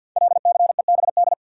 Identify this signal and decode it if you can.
Morse code, heard as a keyed tone, about 47 wpm, 700 Hz. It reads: LCEBD
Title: Code Groups - 5 Letters - 50wpm